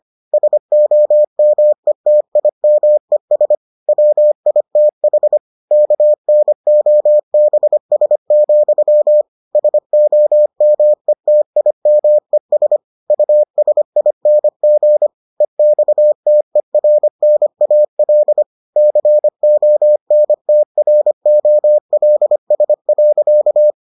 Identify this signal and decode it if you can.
Morse code, heard as a keyed tone, about 25 wpm, 600 Hz. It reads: SOMETIMES WITH KNOBS, SOMETIMES USING EXTERNAL CONTROLS.